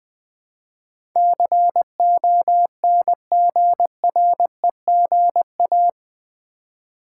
Morse code, heard as a keyed tone, about 20 wpm, 700 Hz. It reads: CONGREGA
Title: Build A Word - 4 Letters At A Time - 12-Letter Words - 20wpm